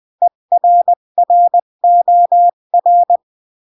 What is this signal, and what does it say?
Morse code, heard as a keyed tone, about 20 wpm, 700 Hz. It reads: ERROR
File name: CW / jodorowsky / 82 commandments 20-15